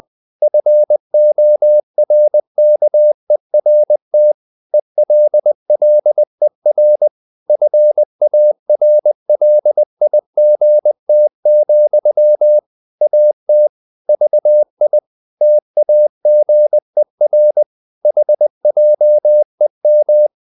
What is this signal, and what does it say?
Morse code, heard as a keyed tone, about 20 wpm, 600 Hz. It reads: FORKERT ELLER FARLIGT, AT VI TAGER HJEM
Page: Ferie 20 wpm